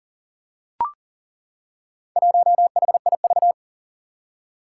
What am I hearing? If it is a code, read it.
Morse code, heard as a keyed tone, about 40 wpm, 700 Hz. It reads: 1HIV
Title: Four Challenging Letter-Number Combinations 40wpm